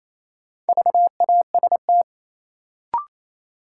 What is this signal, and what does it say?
Morse code, heard as a keyed tone, about 28 wpm, 700 Hz. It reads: VAST
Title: Sets of 4 Words - New Word Drill - 28wpm